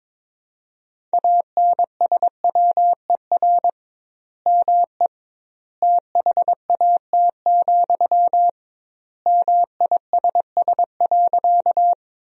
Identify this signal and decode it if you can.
Morse code, heard as a keyed tone, about 22 wpm, 700 Hz. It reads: ANSWER ME THAT, MISS.
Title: 2x Word-Spacing - Sentences from Top 500 Words - 22wpm